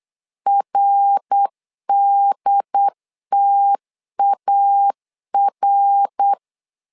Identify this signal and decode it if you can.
Morse code, heard as a keyed tone, about 8 wpm, 800 Hz. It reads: RDTAR
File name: LEZIONI / 3 MWRO / 3 MWR 8wpm 2444